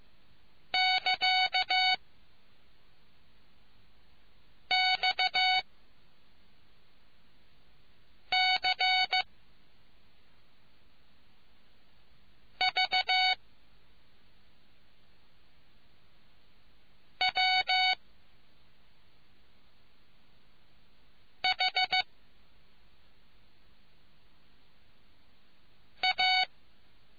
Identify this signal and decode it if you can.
Morse code, heard as a keyed tone, about 15 wpm, 750 Hz. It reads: <KA> X C V W H A